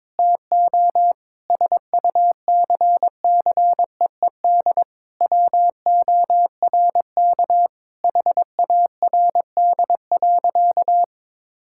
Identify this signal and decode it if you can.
Morse code, heard as a keyed tone, about 22 wpm, 700 Hz. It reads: TO SUCCEED WORK HARD.
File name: practice / 22wpm / Sentences with 3-Word Phrases - 22wpm